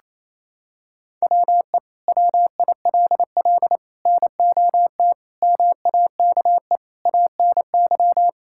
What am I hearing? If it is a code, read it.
Morse code, heard as a keyed tone, about 28 wpm, 700 Hz. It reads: WE WILL NOT MAKE ANY
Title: Sentences from Top 1000 Words - Review Part 5 - 28wpm